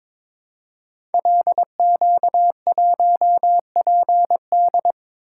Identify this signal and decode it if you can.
Morse code, heard as a keyed tone, about 22 wpm, 700 Hz. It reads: LQ1PD
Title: Code Groups - 5 Letter:Numbers - 22wpm